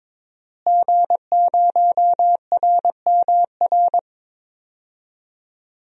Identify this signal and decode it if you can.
Morse code, heard as a keyed tone, about 22 wpm, 700 Hz. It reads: G0RMR
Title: DXCC-100 Call Signs - 5 Characters - Repeated 3x - 22wpm